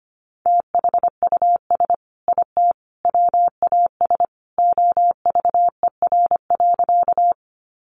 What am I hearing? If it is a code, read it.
Morse code, heard as a keyed tone, about 25 wpm, 700 Hz. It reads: THUS IT WAS OVER.
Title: Sentences from Top 800 Words 25wpm